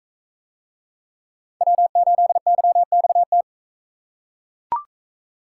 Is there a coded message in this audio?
Yes: Morse code from 1.6 to 3.4 s, heard as a keyed tone, about 41 wpm, 700 Hz. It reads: W8YXT